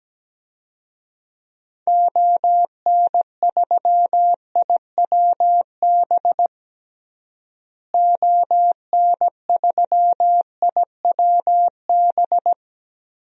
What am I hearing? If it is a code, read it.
Morse code, heard as a keyed tone, about 17 wpm, 700 Hz. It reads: ON3IWB ON3IWB
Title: DXCC-100 Call Signs - 6 Characters - Repeated 3x - 17wpm